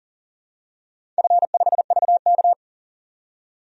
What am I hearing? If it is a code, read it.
Morse code, heard as a keyed tone, about 42 wpm, 700 Hz. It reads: F5VK
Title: DXCC-100 Call Signs - 4 Characters - 40wpm